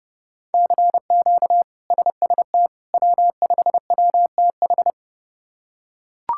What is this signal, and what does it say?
Morse code, heard as a keyed tone, about 30 wpm, 700 Hz. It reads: CQ SST W5WTH
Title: K1USN SST QSO Practice - 30wpm